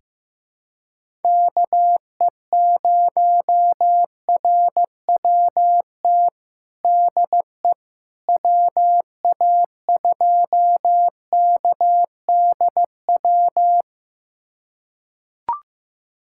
Morse code, heard as a keyed tone, about 15 wpm, 700 Hz. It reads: KE0RWT DE WA2KDW
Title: USA Callsign DE Callsign - 6 Characters - 15wpm